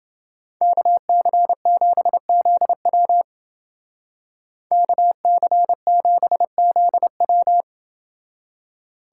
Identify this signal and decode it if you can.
Morse code, heard as a keyed tone, about 29 wpm, 700 Hz. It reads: KC7ZW KC7ZW